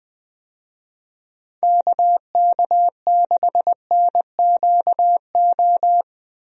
Morse code, heard as a keyed tone, about 20 wpm, 700 Hz. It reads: KK6NQO
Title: USA Call Signs - 6 Characters - Repeated 2x - 20wpm